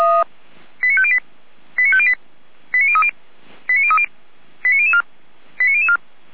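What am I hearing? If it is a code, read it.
Five-tone selective call (ZVEI), heard as quick runs of five tones, tones of about 70 ms. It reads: 89508 89508 89030 89030 89004 89004
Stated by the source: Five-tone ZVEI selective call received by SDR, from feuerwehr-orlamuende (Orlamünde fire brigade) audio server.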